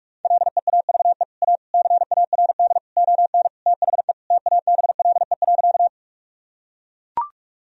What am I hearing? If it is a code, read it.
Morse code, heard as a keyed tone, about 43 wpm, 700 Hz. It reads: LEAVE A CARD ON THE TABLE.
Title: Sentences from Top 1000 Words - Review Part 4 - 45wpm